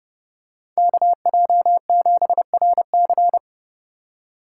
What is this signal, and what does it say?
Morse code, heard as a keyed tone, about 28 wpm, 700 Hz. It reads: KJ7RC